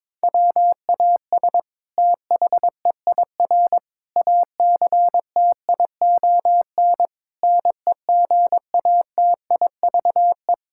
Morse code, heard as a keyed tone, about 22 wpm, 700 Hz. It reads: WAS THEIR ACTION NEGATIVE
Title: Sentences from Top 1000 Words - Review Part 6 - 22wpm